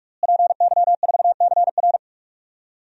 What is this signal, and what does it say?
Morse code, heard as a keyed tone, about 46 wpm, 700 Hz. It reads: PY4CR
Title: DXCC-100 Call Signs - 5 Characters - Repeated 2x - 45wpm